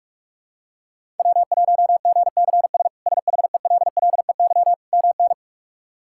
Morse code, heard as a keyed tone, about 41 wpm, 700 Hz. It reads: W1GCS SHELLEY MN
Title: K1USN SST QSO Practice - 45wpm